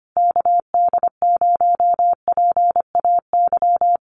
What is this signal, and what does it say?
Morse code, heard as a keyed tone, about 25 wpm, 700 Hz. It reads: KD0PAY
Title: Common QSO Phrases Encore 25wpm